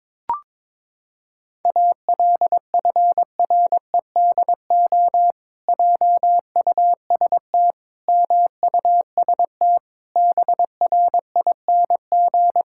Morse code, heard as a keyed tone, about 22 wpm, 700 Hz. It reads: ALFREDO JUST MUST BRING
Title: Pangram Challenge v2 - 22wpm